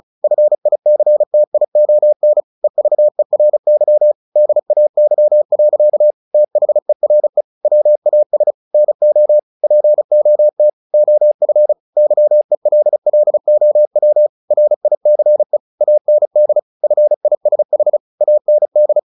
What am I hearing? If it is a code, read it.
Morse code, heard as a keyed tone, about 34 wpm, 600 Hz. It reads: FICTION EVERY DAY. THERE WAS NO POT OF YELLOW RICE AND FISH AND
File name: the old man and the sea / the old man and the sea 1 35wpm